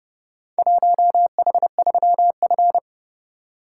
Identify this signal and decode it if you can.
Morse code, heard as a keyed tone, about 30 wpm, 700 Hz. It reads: 1H3F